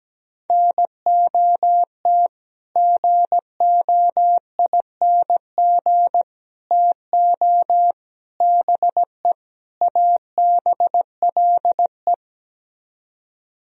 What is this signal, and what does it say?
Morse code, heard as a keyed tone, about 17 wpm, 700 Hz. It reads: NOT GOING TO BE ABLE
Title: Speed-Racing - Sets of 5 Words 17wpm